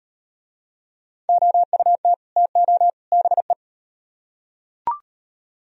Morse code, heard as a keyed tone, about 33 wpm, 700 Hz. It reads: OUT TO BE